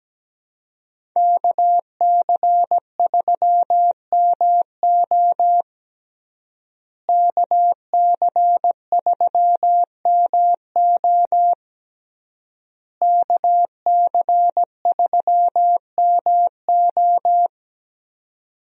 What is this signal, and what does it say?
Morse code, heard as a keyed tone, about 17 wpm, 700 Hz. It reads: KC3MO KC3MO KC3MO
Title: USA Call Signs - 5 Characters - Repeated 3x - 17wpm